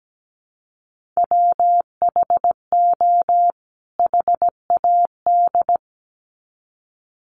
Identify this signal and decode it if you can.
Morse code, heard as a keyed tone, about 17 wpm, 700 Hz. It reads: WHO HAD
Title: Sets of 2 Words 17wpm